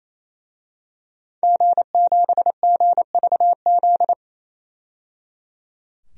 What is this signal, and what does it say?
Morse code, heard as a keyed tone, about 28 wpm, 700 Hz. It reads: G7GVZ
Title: DXCC-100 Call Signs - 5 Characters - 28wpm